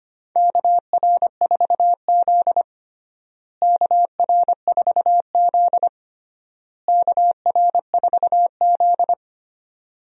Morse code, heard as a keyed tone, about 25 wpm, 700 Hz. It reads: KR4Z KR4Z KR4Z
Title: USA Call Signs - 4 Characters - Repeated 3x - 25wpm